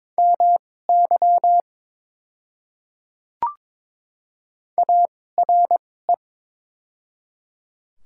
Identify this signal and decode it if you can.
Morse code, heard as a keyed tone, about 22 wpm, spaced as for 16 wpm, 690 Hz. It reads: MY ARE
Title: 2x Character Spacing - Top 500 Words - 22wpm